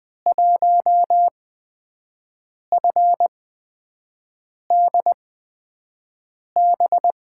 Morse code, heard as a keyed tone, about 20 wpm, spaced as for 5 wpm, 700 Hz. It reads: 1FDB
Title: USA Call Signs - 8x Character Spacing - 20wpm